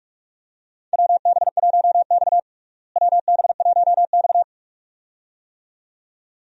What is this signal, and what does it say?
Morse code, heard as a keyed tone, about 46 wpm, 700 Hz. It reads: WB1X WB1X
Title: USA Call Signs - 4 Characters - Repeated 2x - 45wpm